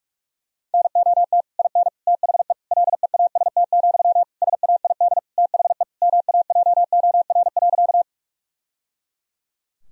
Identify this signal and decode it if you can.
Morse code, heard as a keyed tone, about 46 wpm, 700 Hz. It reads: NOT IN THE LEAST, SAID THE MAJOR.
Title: Sentences from Top 1000 Words - Review Part 4 - 45wpm